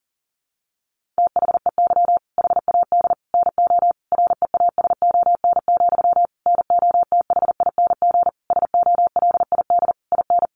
Morse code, heard as a keyed tone, about 40 wpm, 700 Hz. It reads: THEY HAD NO REASON, NOTHING SOLID IN
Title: Sentences from Top 1500 Words 40wpm